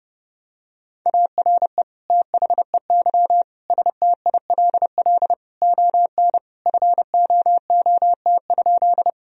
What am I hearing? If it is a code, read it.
Morse code, heard as a keyed tone, about 32 wpm, 700 Hz. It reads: ARE THEY STILL ON FOOT?
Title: Sentences from Top 1000 Words - Review Part 3 - 30wpm